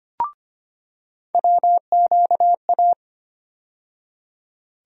Morse code, heard as a keyed tone, about 25 wpm, 700 Hz. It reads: WQA
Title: USA Call Sign Postfixes - 3 Letters 25wpm